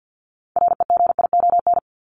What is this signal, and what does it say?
Morse code, heard as a keyed tone, about 50 wpm, 700 Hz. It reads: REGION
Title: Top 1251-1500 Words 50wpm